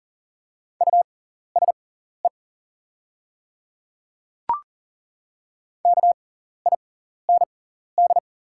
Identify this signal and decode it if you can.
Morse code, heard as a keyed tone, about 39 wpm, spaced as for 14 wpm, 690 Hz. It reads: USE KIND
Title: 6x Character Spacing - Top 500 Words - 40wpm